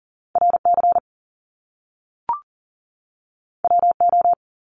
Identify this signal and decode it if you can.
Morse code, heard as a keyed tone, about 39 wpm, 700 Hz. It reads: RC WO
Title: Top Two Letter Combinations in Words 40wpm